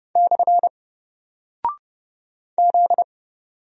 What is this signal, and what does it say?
Morse code, heard as a keyed tone, about 30 wpm, 700 Hz. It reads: / Z
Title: Single Character (Letter:Number:Punctuation:Prosign) Alt 30wpm